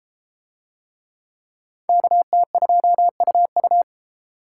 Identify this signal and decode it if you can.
Morse code, heard as a keyed tone, about 32 wpm, 700 Hz. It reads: KT2UU